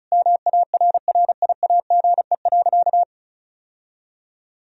Morse code, heard as a keyed tone, about 34 wpm, 690 Hz. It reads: MARRIAGE.
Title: Sentences from Top 1000 Words - Review Part 1 - 35wpm